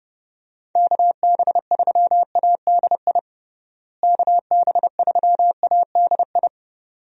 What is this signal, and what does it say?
Morse code, heard as a keyed tone, about 32 wpm, 700 Hz. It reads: KB3ADI KB3ADI
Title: USA Call Signs - 6 Characters - Repeated 2x - 30wpm